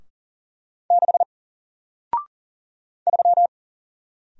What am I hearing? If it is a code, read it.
Morse code, heard as a keyed tone, about 40 wpm, 700 Hz. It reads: 6 3
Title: Single Letter-Number 40wpm